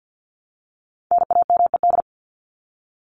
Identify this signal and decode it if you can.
Morse code, heard as a keyed tone, about 50 wpm, 700 Hz. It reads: NAMED
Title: ED Suffix - Top 100 Words with Base Words 50wpm